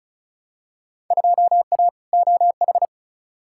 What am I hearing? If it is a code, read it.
Morse code, heard as a keyed tone, about 36 wpm, 700 Hz. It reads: 2A OH
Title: Field Day Exchanges - 35wpm